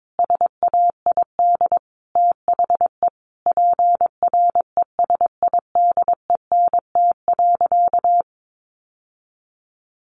Morse code, heard as a keyed tone, about 22 wpm, 700 Hz. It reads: SAID THE PRESIDENT.
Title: Sentences from Top 1000 Words 22wpm